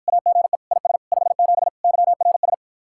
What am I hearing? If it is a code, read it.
Morse code, heard as a keyed tone, about 54 wpm, 700 Hz. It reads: AGE IS 57 YRS